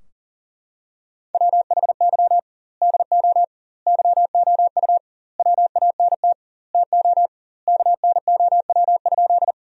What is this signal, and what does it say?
Morse code, heard as a keyed tone, about 40 wpm, 700 Hz. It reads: WHY DO YOU WANT TO KNOW?
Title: 2x Word-Spacing - Sentences from Top 200 Words - 40wpm